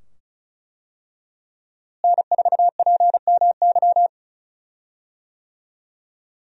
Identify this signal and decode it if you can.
Morse code, heard as a keyed tone, about 34 wpm, 700 Hz. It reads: N4PMY
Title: USA Call Signs - 5 Characters - Repeated 2x - 35wpm